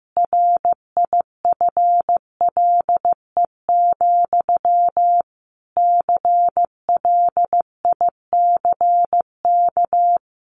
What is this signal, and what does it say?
Morse code, heard as a keyed tone, about 15 wpm, 700 Hz. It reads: RIFLE, CLICK
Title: A Sound Of Thunder 15wpm